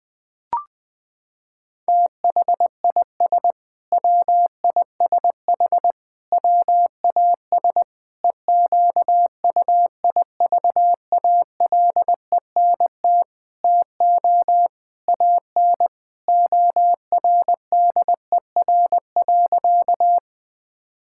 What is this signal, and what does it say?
Morse code, heard as a keyed tone, about 20 wpm, 700 Hz. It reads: THIS WISH WAS EQUIVALENT TO AN ORDER.
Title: Sentences from Top 1600 Words 20wpm